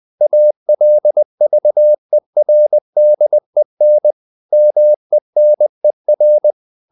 Morse code, heard as a keyed tone, about 20 wpm, 600 Hz. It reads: ALVERDEN MENER
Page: Ferie 20 wpm